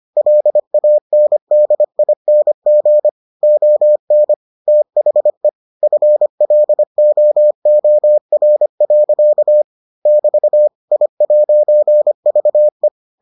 Morse code, heard as a keyed tone, about 25 wpm, 600 Hz. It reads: LANDING ON THE FLOOR. <BT> I'VE